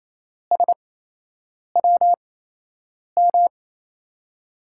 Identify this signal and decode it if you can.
Morse code, heard as a keyed tone, about 28 wpm, spaced as for 8 wpm, 700 Hz. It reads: SWM
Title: USA Call Signs - 5 Characters - 8x Character Spacing - 28wpm